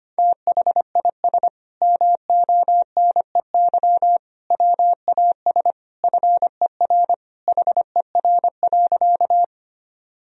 Morse code, heard as a keyed tone, about 25 wpm, 700 Hz. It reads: THIS MONEY WAS FER HER.
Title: Sentences from Top 400 Words and Abbreviations 25wpm